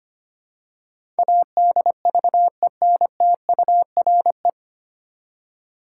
Morse code, heard as a keyed tone, about 25 wpm, 700 Hz. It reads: ADVENTURE
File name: Nine Letter Words - 25wpm